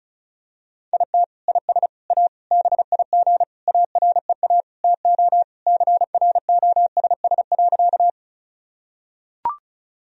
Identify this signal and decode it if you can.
Morse code, heard as a keyed tone, about 36 wpm, 700 Hz. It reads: IT IS A BIG AREA TO CROSS.
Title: Sentences from Top 1000 Words - Review Part 2 - 35wpm